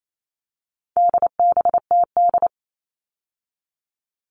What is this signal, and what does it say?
Morse code, heard as a keyed tone, about 28 wpm, 700 Hz. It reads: DBTD
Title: All Cut Numbers - 4-Digit 28wpm